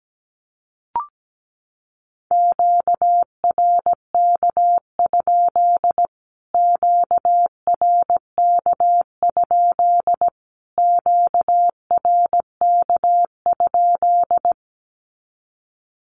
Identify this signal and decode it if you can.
Morse code, heard as a keyed tone, about 17 wpm, 700 Hz. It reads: QRK? QRK? QRK?
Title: Q-Codes Repeated 3x 17wpm